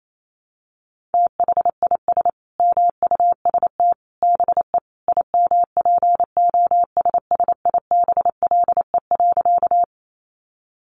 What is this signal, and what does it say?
Morse code, heard as a keyed tone, about 28 wpm, 700 Hz. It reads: THIS MUST BE IMPOSSIBLE.
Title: Sentences from Top 1100 Words 28wpm